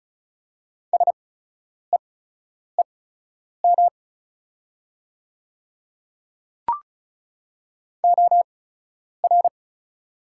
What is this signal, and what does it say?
Morse code, heard as a keyed tone, about 34 wpm, spaced as for 10 wpm, 700 Hz. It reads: SEEM OR
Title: 8x Character Spacing - Top 300 Words - 35wpm